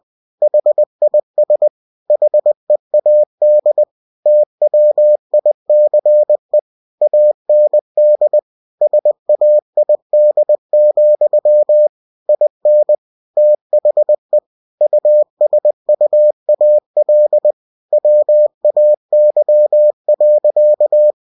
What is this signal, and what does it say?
Morse code, heard as a keyed tone, about 20 wpm, 600 Hz. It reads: HIS HEAD TWICE AND SAID, IN THE USUAL WAY.